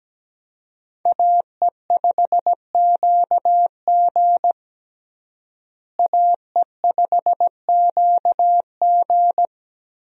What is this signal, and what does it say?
Morse code, heard as a keyed tone, about 17 wpm, 700 Hz. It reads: AE5QG AE5QG
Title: USA Call Signs - 5 Characters - Repeated 3x - 17wpm